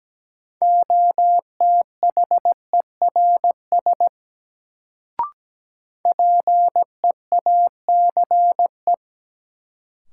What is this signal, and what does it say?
Morse code, heard as a keyed tone, about 17 wpm, 700 Hz. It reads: OTHERS PEACE
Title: Sentences from Top 1500 Words - New Word Drill Alt - 17wpm